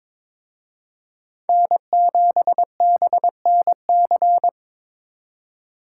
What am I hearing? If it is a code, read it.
Morse code, heard as a keyed tone, about 26 wpm, 700 Hz. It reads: N7BNC
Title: Speed Racing - USA Call Signs 22wpm